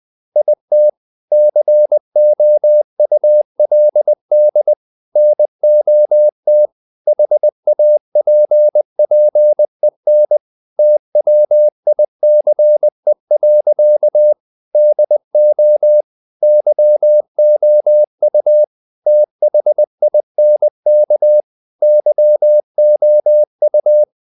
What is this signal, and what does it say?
Morse code, heard as a keyed tone, about 20 wpm, 600 Hz. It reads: IT COULD NOT HAPPEN TWICE. DO YOU THINK YOU
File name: the old man and the sea / the old man and the sea part 1 20wpm